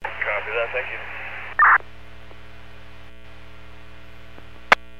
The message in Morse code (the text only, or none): none